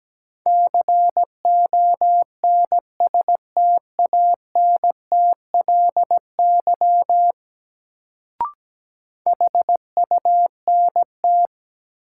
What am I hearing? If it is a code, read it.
Morse code, heard as a keyed tone, about 17 wpm, 700 Hz. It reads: CONSTANTLY HUNT
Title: Sentences from Top 1600 Words - New Word Drill Alt - 17wpm